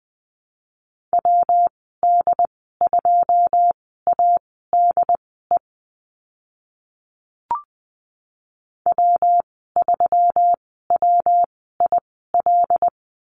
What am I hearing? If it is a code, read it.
Morse code, heard as a keyed tone, about 20 wpm, spaced as for 14 wpm, 700 Hz. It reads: WD2ADE W3WIL
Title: 122 Lesson 20g - Number 3 - Call Signs 20wpm